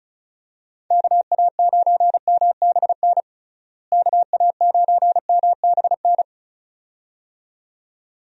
Morse code, heard as a keyed tone, about 36 wpm, 700 Hz. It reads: KA9MBN KA9MBN
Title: USA Call Signs - 6 Characters - Repeated 2x - 35wpm